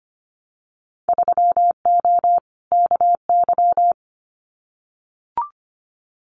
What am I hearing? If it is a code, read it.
Morse code, heard as a keyed tone, about 25 wpm, 700 Hz. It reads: 3O KY
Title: Winter Field Day Exchanges 25wpm